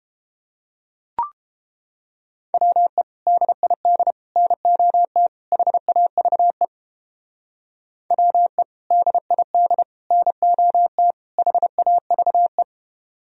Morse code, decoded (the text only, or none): WE DID NOT HAVE WE DID NOT HAVE